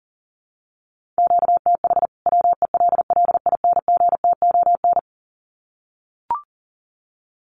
Morse code, heard as a keyed tone, about 40 wpm, 700 Hz. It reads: QTH WELLINGTON
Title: Common QSO Phrases 40wpm